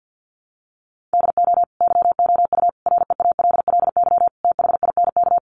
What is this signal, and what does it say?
Morse code, heard as a keyed tone, about 50 wpm, 700 Hz. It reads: DO YOU REALLY THINK
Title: Sentences from Top 600 Words 50wpm